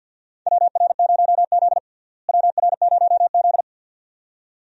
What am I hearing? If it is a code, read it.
Morse code, heard as a keyed tone, about 47 wpm, 700 Hz. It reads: WR0Z WR0Z
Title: USA Call Signs - 4 Characters - Repeated 2x - 50wpm